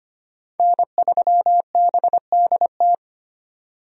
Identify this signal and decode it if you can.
Morse code, heard as a keyed tone, about 25 wpm, 700 Hz. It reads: N3BDT
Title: USA Call Signs - 5 Characters - 25wpm